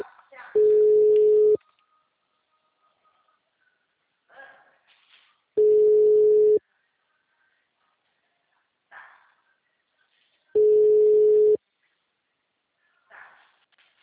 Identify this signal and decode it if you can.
Ringback tone (Europe, 425 Hz), with speech.
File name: out- -2001- - - .43363